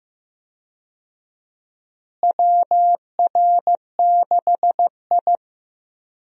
Morse code, heard as a keyed tone, about 16 wpm, 700 Hz. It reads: WR6I
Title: USA Call Signs - 4 Characters - Repeated 2x - 15wpm